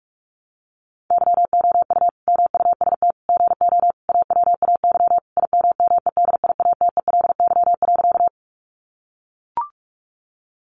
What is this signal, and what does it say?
Morse code, heard as a keyed tone, about 44 wpm, 700 Hz. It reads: YOU MUST GO AWAY IMMEDIATELY.
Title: Sentences from Top 1100 Words 45wpm